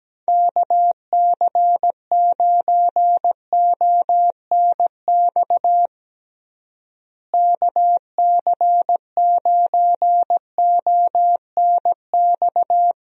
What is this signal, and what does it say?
Morse code, heard as a keyed tone, about 17 wpm, 700 Hz. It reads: KC9ONX KC9ONX
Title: USA Call Signs - 6 Characters - Repeated 3x - 17wpm